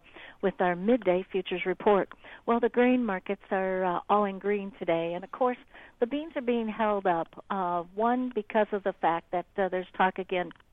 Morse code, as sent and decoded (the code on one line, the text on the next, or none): none